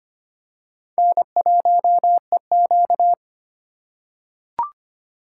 Morse code, heard as a keyed tone, about 25 wpm, 700 Hz. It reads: N1EQ
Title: USA Call Signs - 4 Characters - 4x Character Spacing - 25wpm